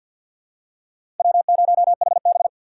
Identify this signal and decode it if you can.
Morse code, heard as a keyed tone, about 47 wpm, 700 Hz. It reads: W0HB
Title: USA Call Signs - 4 Characters - 2x Character Spacing - 50wpm